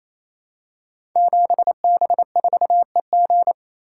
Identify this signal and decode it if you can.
Morse code, heard as a keyed tone, about 28 wpm, 700 Hz. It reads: 7B4EG
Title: Code Groups - 5 Letter:Numbers - 28wpm